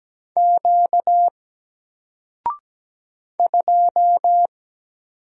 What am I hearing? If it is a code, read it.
Morse code, heard as a keyed tone, about 17 wpm, 700 Hz. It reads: Q 2
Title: Single Letter-Number 17wpm